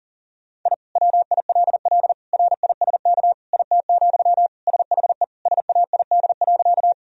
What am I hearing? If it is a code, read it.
Morse code, heard as a keyed tone, about 40 wpm, 700 Hz. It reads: I WILL RISK IT, SHE SAID.
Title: Sentences from Top 1000 Words - Review Part 4 - 40wpm